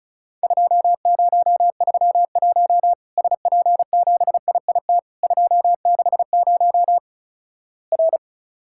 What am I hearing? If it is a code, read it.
Morse code, heard as a keyed tone, about 36 wpm, 700 Hz. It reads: 2031 SP7IIT 260 R
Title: Worked All Europe DX Contest - QTC Exchange - 35wpm